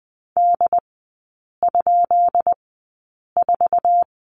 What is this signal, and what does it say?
Morse code, heard as a keyed tone, about 20 wpm, spaced as for 8 wpm, 700 Hz. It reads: D?4
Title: 133 Lesson 22e - Question Mark - Three Characters 20wpm